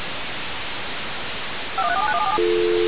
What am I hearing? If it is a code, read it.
DTMF, heard as telephone touch tones, tones of about 60 ms. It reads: 5134791747
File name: radiocap / REC z